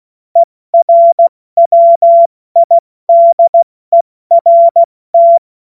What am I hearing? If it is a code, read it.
Morse code, heard as a keyed tone, about 16 wpm, spaced as for 14 wpm, 680 Hz. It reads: ERWIDERT
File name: Morsekurs / Witze / Witz 30. Dezember 2025 70 F80